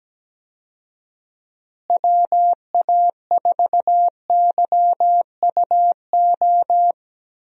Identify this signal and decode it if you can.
Morse code, heard as a keyed tone, about 17 wpm, 700 Hz. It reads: WA4YUO
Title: USA Call Signs - 6 Characters - Repeated 2x - 17wpm